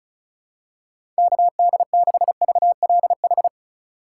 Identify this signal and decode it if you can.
Morse code, heard as a keyed tone, about 36 wpm, 700 Hz. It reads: KD6VLH
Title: USA Call Signs - 2x Character Spacing - 35wpm